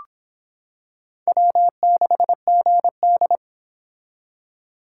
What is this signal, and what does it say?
Morse code, heard as a keyed tone, about 25 wpm, 700 Hz. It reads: W6GD